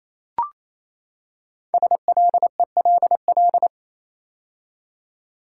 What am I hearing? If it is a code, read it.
Morse code, heard as a keyed tone, about 27 wpm, 700 Hz. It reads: SLELL
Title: Code Groups - 5 Letters - 28wpm